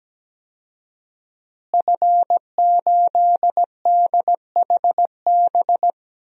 Morse code, heard as a keyed tone, about 17 wpm, 700 Hz. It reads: F8DHB
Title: DXCC-100 Call Signs - 5 Characters - Repeated 3x - 17wpm